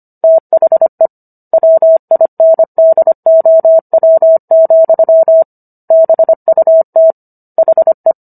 Morse code, heard as a keyed tone, about 25 wpm, 650 Hz. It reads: THE WINDOW, BUT HE